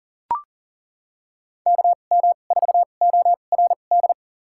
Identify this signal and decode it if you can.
Morse code, heard as a keyed tone, about 40 wpm, spaced as for 29 wpm, 700 Hz. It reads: KM4ORD
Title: USA Call Signs - 6 Characters - 2x Character Spacing - 40wpm